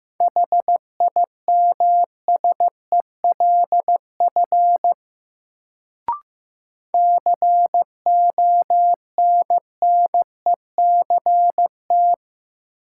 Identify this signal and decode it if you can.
Morse code, heard as a keyed tone, about 15 wpm, 700 Hz. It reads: HIMSELF CONNECT
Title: Sentences from Top 1500 Words - New Word Drill - 15wpm